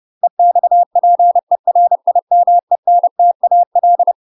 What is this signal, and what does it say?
Morse code, heard as a keyed tone, about 30 wpm, 700 Hz. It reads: EXPERIMENTAL
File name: UK MASTER QSOs TRAINING/30WPM